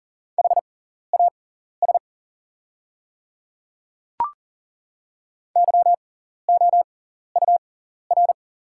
Morse code, heard as a keyed tone, about 40 wpm, spaced as for 14 wpm, 690 Hz. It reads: HAS YOUR
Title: 6x Character Spacing - Top 100 Words - 40wpm